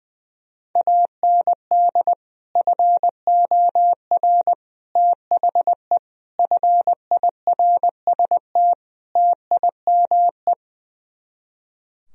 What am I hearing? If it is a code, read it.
Morse code, heard as a keyed tone, about 20 wpm, 700 Hz. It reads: AND FOR THE FIRST TIME
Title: Speed-Racing - Sets of 5 Words 20wpm